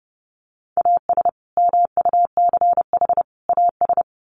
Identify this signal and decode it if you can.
Morse code, heard as a keyed tone, about 30 wpm, 700 Hz. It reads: AS MUCH AS
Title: Sets of 3 Words 30wpm